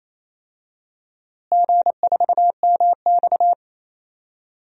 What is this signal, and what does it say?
Morse code, heard as a keyed tone, about 28 wpm, 700 Hz. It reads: G4MX
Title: DXCC-100 Call Signs - 4 Characters - Repeated 3x - 28wpm